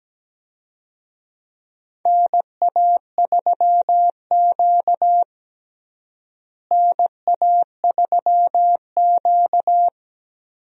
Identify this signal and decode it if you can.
Morse code, heard as a keyed tone, about 17 wpm, 700 Hz. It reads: NA3Q NA3Q
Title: USA Call Signs - 4 Characters - Repeated 2x - 17wpm